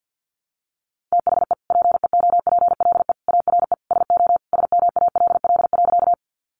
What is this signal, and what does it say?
Morse code, heard as a keyed tone, about 50 wpm, 700 Hz. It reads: THE PEOPLE ARE SO SMALL.
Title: Additional Easy Phrases 50wpm